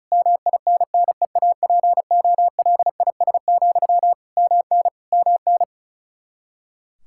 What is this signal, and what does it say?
Morse code, heard as a keyed tone, about 36 wpm, 700 Hz. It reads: MINNEAPOLIS, MN MN
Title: Practice QSO - Exchange 1 - Beginning - 35wpm